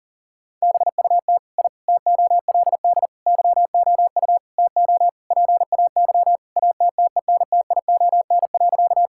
Morse code, heard as a keyed tone, about 40 wpm, 700 Hz. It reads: BUT I TOLD YOU TO PAY ATTENTION.
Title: Sentences from Top 1000 Words - Review Part 6 - 40wpm